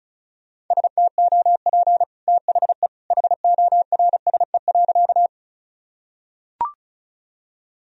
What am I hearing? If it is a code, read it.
Morse code, heard as a keyed tone, about 36 wpm, 700 Hz. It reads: STOP THE HORSE.
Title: Sentences from Top 1000 Words - Review Part 1 - 35wpm